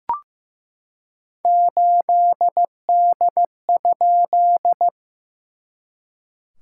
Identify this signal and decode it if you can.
Morse code, heard as a keyed tone, about 14 wpm, 700 Hz. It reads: 8D?
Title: SOTA QSO Practice 15wpm